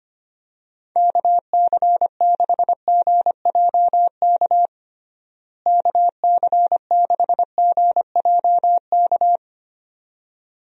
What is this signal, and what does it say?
Morse code, heard as a keyed tone, about 25 wpm, 700 Hz. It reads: KC6GJK KC6GJK
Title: USA Call Signs - 6 Characters - Repeated 2x - 25wpm